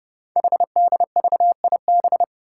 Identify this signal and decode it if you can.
Morse code, heard as a keyed tone, about 28 wpm, 690 Hz. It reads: HDVIB
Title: Code Groups - 5 Letters - 30wpm